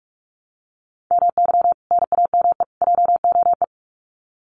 Morse code, heard as a keyed tone, about 45 wpm, 700 Hz. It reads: MY NAME JOE
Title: Common QSO Phrases 45wpm